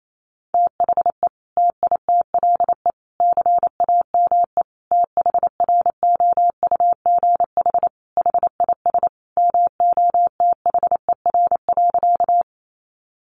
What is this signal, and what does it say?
Morse code, heard as a keyed tone, about 28 wpm, 700 Hz. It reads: THE TITLE CAME THROUGH HIS MOTHER.
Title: Sentences from Top 1200 Words 28wpm